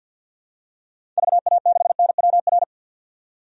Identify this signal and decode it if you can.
Morse code, heard as a keyed tone, about 50 wpm, 700 Hz. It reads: VA6NWR
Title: DXCC-100 Call Signs - 6 Characters - 50wpm